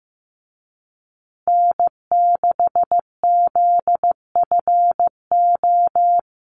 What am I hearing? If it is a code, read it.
Morse code, heard as a keyed tone, about 15 wpm, 700 Hz. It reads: N6ZFO
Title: Call Signs Repeated 2x - 15wpm